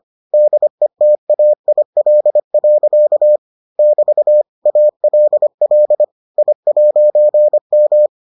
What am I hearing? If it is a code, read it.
Morse code, heard as a keyed tone, about 25 wpm, 600 Hz. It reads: DETAIL. <BT> ALL I'M